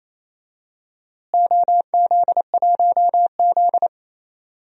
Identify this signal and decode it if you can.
Morse code, heard as a keyed tone, about 28 wpm, 700 Hz. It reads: OZ1Z
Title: DXCC-100 Call Signs - 4 Characters - Repeated 2x - 28wpm